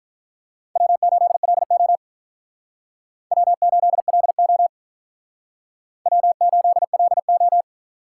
Morse code, heard as a keyed tone, about 44 wpm, 700 Hz. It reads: W8LO W8LO W8LO